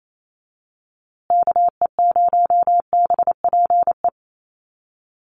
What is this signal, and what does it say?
Morse code, heard as a keyed tone, about 28 wpm, 700 Hz. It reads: KE0BPE
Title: USA Call Signs Repeated 3x 28wpm